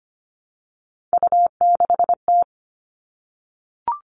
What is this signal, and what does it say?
Morse code, heard as a keyed tone, about 25 wpm, 700 Hz. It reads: U6T
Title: All Cut Numbers - 3-Digit 25wpm